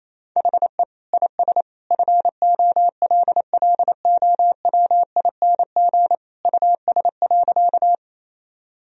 Morse code, heard as a keyed tone, about 28 wpm, 700 Hz. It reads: HE IS FOLLOWING US.
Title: Sentences from Top 1000 Words - Review Part 4 - 28wpm